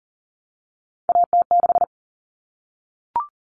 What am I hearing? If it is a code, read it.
Morse code, heard as a keyed tone, about 40 wpm, 700 Hz. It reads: AT6
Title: Common Cut Numbers - 3-Digit 40wpm